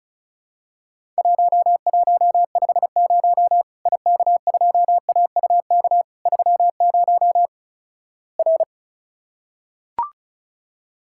Morse code, heard as a keyed tone, about 36 wpm, 700 Hz. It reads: 1150 IK2AUK 30 R
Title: Worked All Europe DX Contest - QTC Exchange - 35wpm